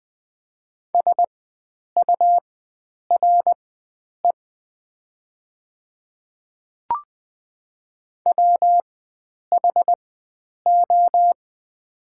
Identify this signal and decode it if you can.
Morse code, heard as a keyed tone, about 20 wpm, spaced as for 9 wpm, 700 Hz. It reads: SURE WHO
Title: 4x Character Spacing - Top 500 Words - 20wpm